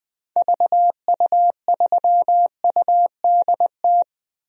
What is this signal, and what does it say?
Morse code, heard as a keyed tone, about 20 wpm, 700 Hz. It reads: VU3UDT
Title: DXCC-100 Call Signs - 6 Characters - 20wpm